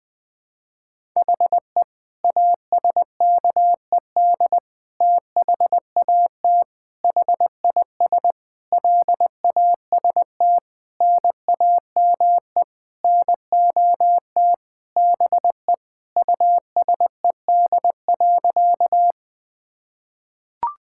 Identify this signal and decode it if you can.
Morse code, heard as a keyed tone, about 20 wpm, 700 Hz. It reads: HE ASKED THAT HIS LAST NAME NOT BE USED.
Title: Sentences with 9-Word Phrases - Spoken Beginning - 20wpm